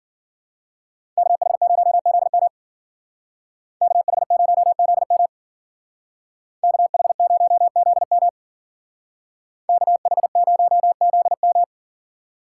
Morse code, heard as a keyed tone, about 52 wpm, 700 Hz. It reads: KH0ZM KH0ZM KH0ZM KH0ZM